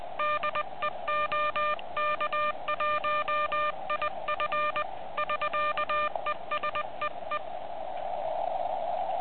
Morse code, heard as a keyed tone, about 20 wpm, 1200 Hz, with noise. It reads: DE OK1IF <SK> E SEE
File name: ARCHIV / QCX / vu2gsm2